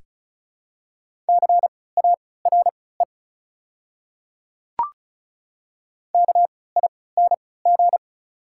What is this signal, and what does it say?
Morse code, heard as a keyed tone, about 36 wpm, spaced as for 20 wpm, 690 Hz. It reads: CARE KING
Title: 3x Character Spacing - Top 500 Words - 35wpm